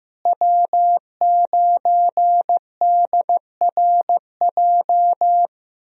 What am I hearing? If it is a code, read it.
Morse code, heard as a keyed tone, about 15 wpm, 700 Hz. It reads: W9DRJ
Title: USA Call Signs - 5 Characters - Repeated 3x - 15wpm